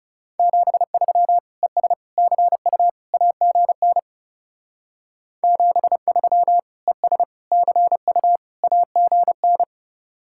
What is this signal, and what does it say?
Morse code, heard as a keyed tone, about 36 wpm, 700 Hz. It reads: 73 ES CU AGN 73 ES CU AGN
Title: Speed-Racing Common QSO Phrases Encore - 30wpm